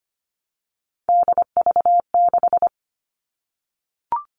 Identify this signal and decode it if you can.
Morse code, heard as a keyed tone, about 25 wpm, 700 Hz. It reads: DV6
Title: All Cut Numbers - 3-Digit 25wpm